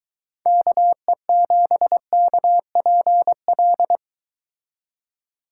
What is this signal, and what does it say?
Morse code, heard as a keyed tone, about 23 wpm, 700 Hz. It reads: KE7KPL